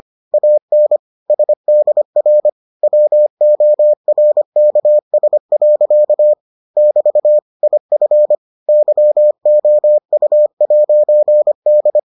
Morse code, heard as a keyed tone, about 25 wpm, 600 Hz. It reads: AN SDR WORKS. <BT> IF YOU'D